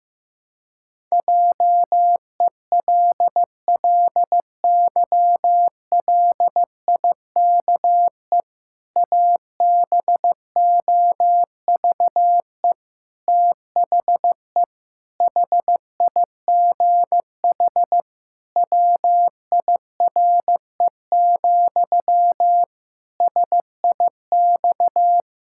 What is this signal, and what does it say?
Morse code, heard as a keyed tone, about 16 wpm, 700 Hz. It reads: JELLYLIKE ABOVE THE HIGH WIRE, SIX
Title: Pangram Challenge v2 - 15wpm